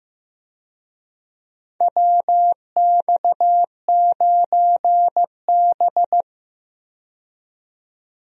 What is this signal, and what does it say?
Morse code, heard as a keyed tone, about 15 wpm, 700 Hz. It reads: WX9B
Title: USA Call Signs - 4 Characters - 15wpm